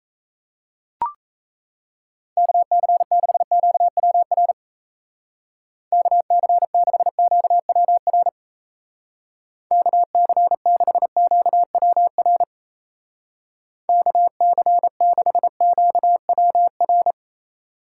Morse code, heard as a keyed tone, about 35 wpm, 700 Hz. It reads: KC6QWR KC6QWR KC6QWR KC6QWR
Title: Speed Racing - USA 2x3 Call Signs 28wpm